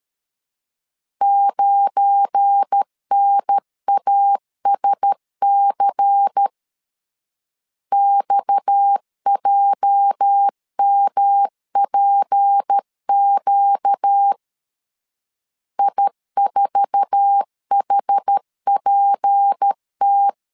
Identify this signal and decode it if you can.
Morse code, heard as a keyed tone, about 13 wpm, 800 Hz. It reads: 9NASC XJMPQ I4HPT
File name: LEZIONI / 8 YJ45 / 8 YJ4 12wpm 2151